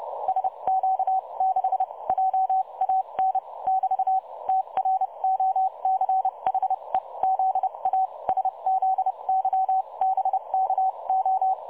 Morse code, heard as a keyed tone, about 31 wpm, spaced as for 25 wpm, 760 Hz. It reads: SQ6JAN <BT> TROCHE ZA SZYBKO
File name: dlazaawansowanych / nagraniazpasma / websdr recording 3565.0kHz